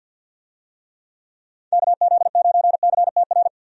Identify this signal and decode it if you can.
Morse code, heard as a keyed tone, about 49 wpm, 700 Hz. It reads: KZ9CTR
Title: USA Call Signs - 6 Characters - 50wpm